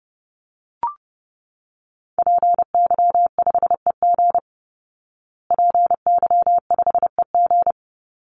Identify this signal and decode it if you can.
Morse code, heard as a keyed tone, about 30 wpm, 700 Hz. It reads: PY5EG PY5EG
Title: Call Signs Repeated 2x - 30wpm